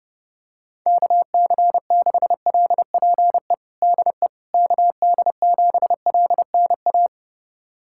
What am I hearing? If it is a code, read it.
Morse code, heard as a keyed tone, about 32 wpm, 690 Hz. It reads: KC6LPE DE KD7LNA
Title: USA Callsign DE Callsign - 6 Characters - 30wpm